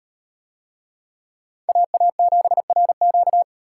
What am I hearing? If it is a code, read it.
Morse code, heard as a keyed tone, about 35 wpm, 700 Hz. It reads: AA7RQ